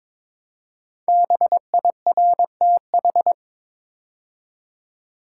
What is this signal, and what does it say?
Morse code, heard as a keyed tone, about 22 wpm, 700 Hz. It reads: BIRTH
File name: 5 Letter Words - First 3 Letters Given - 22wpm